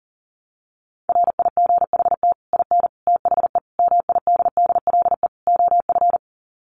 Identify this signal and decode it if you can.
Morse code, heard as a keyed tone, about 37 wpm, 700 Hz. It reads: RIGHT IN THE MIDDLE OF
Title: Sets of 5 Words 40wpm